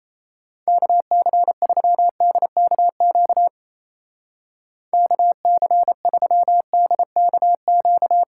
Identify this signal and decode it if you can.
Morse code, heard as a keyed tone, about 35 wpm, 700 Hz. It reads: KC3DKQ KC3DKQ